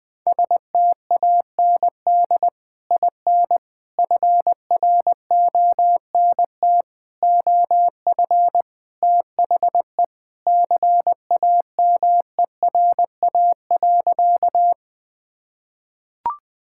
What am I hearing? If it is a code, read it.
Morse code, heard as a keyed tone, about 20 wpm, 700 Hz. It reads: STAND IN FRONT OF THE CAMERA.
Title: Sentences with 3-Word Phrases - 20wpm